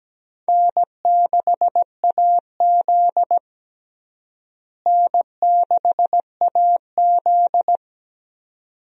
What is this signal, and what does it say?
Morse code, heard as a keyed tone, about 17 wpm, 700 Hz. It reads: N6AZ N6AZ
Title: USA Call Signs - 4 Characters - Repeated 2x - 17wpm